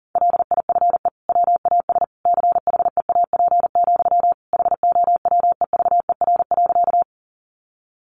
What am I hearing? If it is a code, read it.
Morse code, heard as a keyed tone, about 40 wpm, 700 Hz. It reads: LIFE WAS CHEAP, HOWEVER.
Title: Sentences from Top 1200 Words 40wpm